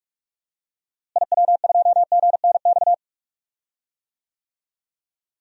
Morse code, heard as a keyed tone, about 42 wpm, 700 Hz. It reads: IW2GNX
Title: DXCC-100 Call Signs - 6 Characters - Repeated 3x - 45wpm